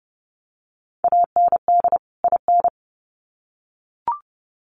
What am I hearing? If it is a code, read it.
Morse code, heard as a keyed tone, about 30 wpm, 700 Hz. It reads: AND IN
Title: Sets of 2 Words 30wpm